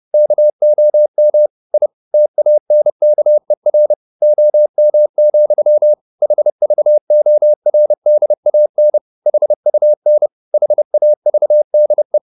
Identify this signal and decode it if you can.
Morse code, heard as a keyed tone, about 30 wpm, 600 Hz. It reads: KOM I TANKER OM, HVORDAN HUN HAVDE
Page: Ferie 30 wpm